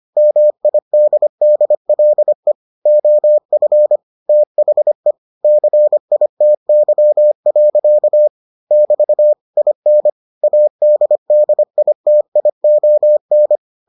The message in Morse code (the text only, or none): MIDDLE OF THE CITY. <BT> IN ADDITION